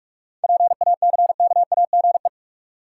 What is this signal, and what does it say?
Morse code, heard as a keyed tone, about 46 wpm, 700 Hz. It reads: PACKAGE
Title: Seven Letter Words - 45wpm